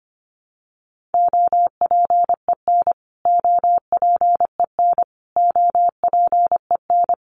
Morse code, heard as a keyed tone, about 25 wpm, 700 Hz. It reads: OPEN OPEN OPEN
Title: Top 500 Words Repeated 25wpm